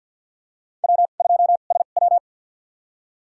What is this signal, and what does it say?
Morse code, heard as a keyed tone, about 49 wpm, spaced as for 35 wpm, 700 Hz. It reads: W2SW
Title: USA Call Signs - 4 Characters - 2x Character Spacing - 50wpm